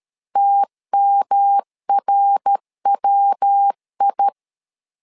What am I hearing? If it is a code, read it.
Morse code, heard as a keyed tone, about 13 wpm, 800 Hz. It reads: TMRWI
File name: LEZIONI / 4 Q1S3 12wpm 2023